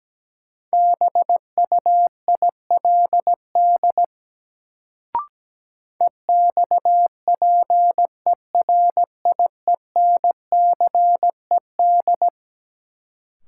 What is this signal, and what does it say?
Morse code, heard as a keyed tone, about 17 wpm, 700 Hz. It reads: BUILD EXPERIENCED
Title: Sentences from Top 1100 Words - New Word Drill Alt - 17wpm